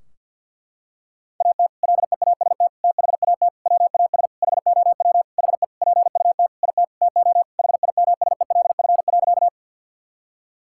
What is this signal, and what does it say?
Morse code, heard as a keyed tone, about 51 wpm, 700 Hz. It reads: AT LEAST THAT WAS HOW HE PUT IT TO HIMSELF.
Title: Sentences from Top 1600 Words 50wpm